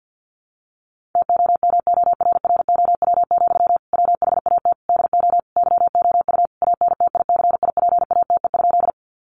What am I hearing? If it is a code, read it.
Morse code, heard as a keyed tone, about 50 wpm, 700 Hz. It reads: TOMORROW, WHAT DO YOU ANTICIPATE?
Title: Sentences from Top 1500 Words 50wpm